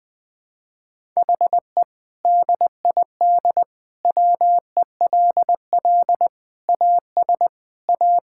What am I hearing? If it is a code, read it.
Morse code, heard as a keyed tone, about 20 wpm, 700 Hz. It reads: HE DID WELL AS A
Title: Sentences with 2-Word Phrases - 20wpm